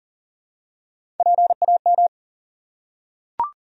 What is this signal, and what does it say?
Morse code, heard as a keyed tone, about 37 wpm, 700 Hz. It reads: PAM
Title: USA Common Names - Max 5 Letters - 40wpm